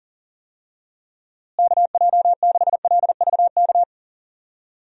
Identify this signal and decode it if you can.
Morse code, heard as a keyed tone, about 40 wpm, 700 Hz. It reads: KJ6LVK
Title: USA Call Signs - 6 Characters - Repeated 3x - 40wpm